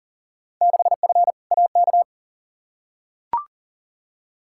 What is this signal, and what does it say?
Morse code, heard as a keyed tone, about 40 wpm, 700 Hz. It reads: 6F AK
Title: Field Day Exchanges Encore - 40wpm